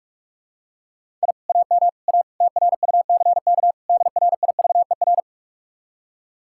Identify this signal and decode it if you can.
Morse code, heard as a keyed tone, about 46 wpm, 700 Hz. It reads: I AM A TRUCK DRIVER